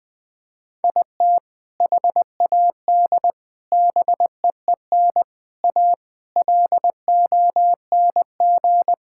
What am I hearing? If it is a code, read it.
Morse code, heard as a keyed tone, about 20 wpm, 700 Hz. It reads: IT HAD BEEN A LONG
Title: Sentences with 2-Word Phrases - 20wpm